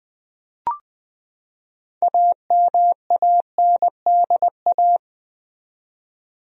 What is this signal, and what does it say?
Morse code, heard as a keyed tone, about 21 wpm, 700 Hz. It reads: AMANDA
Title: USA Common Names - 20wpm